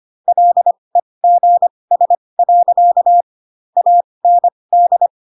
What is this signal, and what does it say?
Morse code, heard as a keyed tone, about 25 wpm, spaced as for 20 wpm, 700 Hz. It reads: LEGS. AND
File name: Pinocchio / Pinocchio ch2 15 25 ik1hge